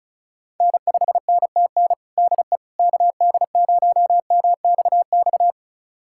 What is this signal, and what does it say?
Morse code, heard as a keyed tone, about 34 wpm, 700 Hz. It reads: N5NTN DE KD0MXX
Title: USA Callsign DE Callsign 35wpm